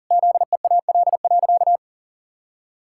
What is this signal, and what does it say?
Morse code, heard as a keyed tone, about 40 wpm, 700 Hz. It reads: ZEAL.
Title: Pangram Challenge v2 - 40wpm